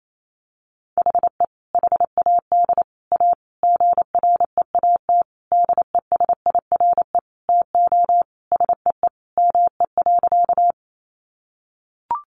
Morse code, heard as a keyed tone, about 28 wpm, 700 Hz. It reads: HE HAD A GREAT DESIRE TO SEE ME.
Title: Sentences from Top 1100 Words 28wpm